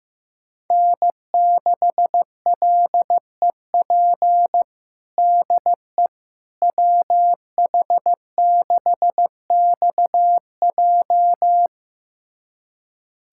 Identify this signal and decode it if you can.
Morse code, heard as a keyed tone, about 15 wpm, 700 Hz. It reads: N6LEP DE WH6XJ
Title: USA Callsign DE Callsign - 5 Characters - 15wpm